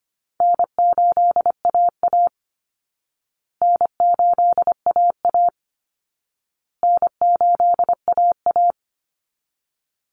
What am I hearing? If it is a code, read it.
Morse code, heard as a keyed tone, about 25 wpm, 700 Hz. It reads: N8AA N8AA N8AA
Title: Easy Call Signs Repeated 3x - 25wpm